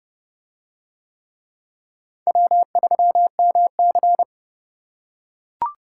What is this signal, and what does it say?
Morse code, heard as a keyed tone, about 29 wpm, 700 Hz. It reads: W3MC